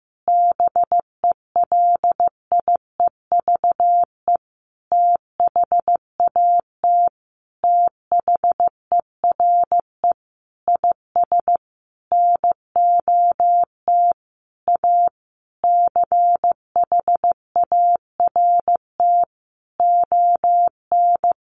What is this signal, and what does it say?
Morse code, heard as a keyed tone, about 15 wpm, 700 Hz. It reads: BELIEVE THAT THERE IS NOT A CHART ON
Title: Sentences from Top 1300 Words 15wpm